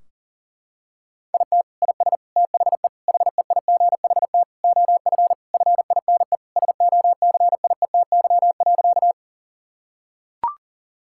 Morse code, heard as a keyed tone, about 37 wpm, 700 Hz. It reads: IT IS THE HEIGHT OF FINE SOCIETY.
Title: Sentences from Top 1600 Words 40wpm